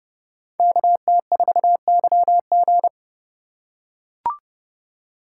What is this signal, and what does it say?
Morse code, heard as a keyed tone, about 29 wpm, 700 Hz. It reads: KT4YG